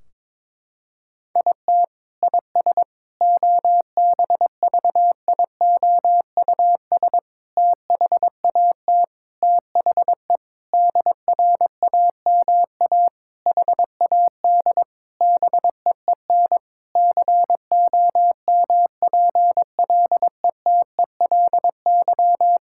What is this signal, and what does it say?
Morse code, heard as a keyed tone, about 22 wpm, 700 Hz. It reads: IT IS OBVIOUS THAT THE DRAMA HAD BEEN COMPLETELY
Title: Sentences from Top 1600 Words 22wpm